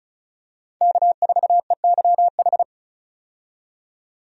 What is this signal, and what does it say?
Morse code, heard as a keyed tone, about 36 wpm, 700 Hz. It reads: K4EYH
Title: USA Call Signs - 5 Characters - 35wpm